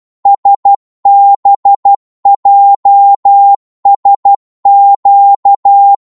Morse code, heard as a keyed tone, about 12 wpm, 800 Hz. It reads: SBJSQ
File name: Morse / ltr-12wpm